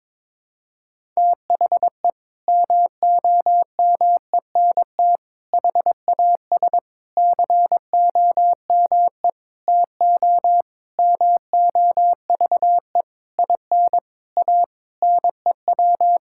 Morse code, heard as a keyed tone, about 22 wpm, 700 Hz. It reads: THE MOMENT HAS COME TO MOVE IN A NEW
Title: Sentences from Top 1000 Words - Review Part 5 - 22wpm